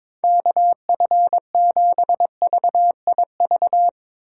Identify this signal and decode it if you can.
Morse code, heard as a keyed tone, about 22 wpm, 700 Hz. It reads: KF7VIV
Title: USA Call Signs - 3x Character Spacing - 22wpm